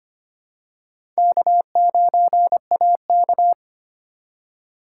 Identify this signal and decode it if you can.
Morse code, heard as a keyed tone, about 25 wpm, 700 Hz. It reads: K9AK
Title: USA Call Signs - 3x Character Spacing - 25wpm